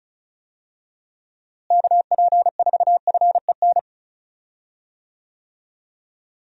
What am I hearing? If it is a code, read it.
Morse code, heard as a keyed tone, about 36 wpm, 700 Hz. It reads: KP4FEN
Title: DXCC-100 Call Signs - 6 Characters - Repeated 2x - 35wpm